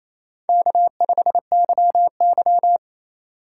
Morse code, heard as a keyed tone, about 28 wpm, 700 Hz. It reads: K5YY
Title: USA Call Signs - 4 Characters - 2x Character Spacing - 28wpm